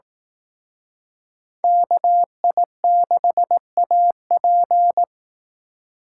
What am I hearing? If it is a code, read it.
Morse code, heard as a keyed tone, about 18 wpm, 700 Hz. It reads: KI6AP
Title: Speed Racing - USA 2x2 Call Signs 15wpm